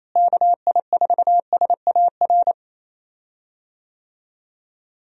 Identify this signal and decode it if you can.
Morse code, heard as a keyed tone, about 28 wpm, 700 Hz. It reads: KI4SAR
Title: USA Call Signs - 6 Characters - Repeated 3x - 28wpm